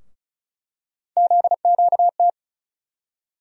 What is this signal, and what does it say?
Morse code, heard as a keyed tone, about 36 wpm, 700 Hz. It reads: ZQT
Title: USA Call Sign Postfixes - 3 Letters 35wpm